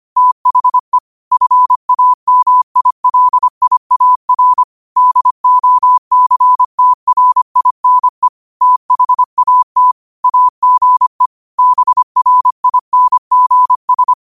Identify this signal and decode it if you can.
Morse code, heard as a keyed tone, about 25 wpm, 1000 Hz. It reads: THE FAMILIAR DOCTRINE THAT AGE BRINGS
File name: qotdfiles / 25wpm